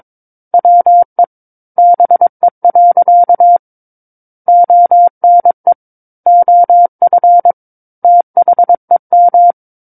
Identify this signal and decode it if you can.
Morse code, heard as a keyed tone, about 22 wpm, 700 Hz. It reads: WE BE. ONE OF THEM
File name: Book One / 13-Chapter 13 - 22 WPM